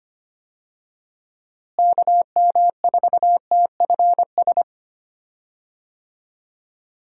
Morse code, heard as a keyed tone, about 25 wpm, 700 Hz. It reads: KM4TFS
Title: USA Call Signs - 6 Characters - Repeated 3x - 25wpm